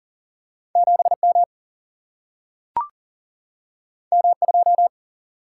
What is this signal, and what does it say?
Morse code, heard as a keyed tone, about 38 wpm, 700 Hz. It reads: 7 M M 2
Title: Single-Letter Single-Number 40wpm